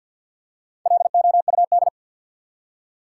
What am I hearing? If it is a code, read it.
Morse code, heard as a keyed tone, about 52 wpm, 700 Hz. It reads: LOUD
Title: Four Letter Words - 50wpm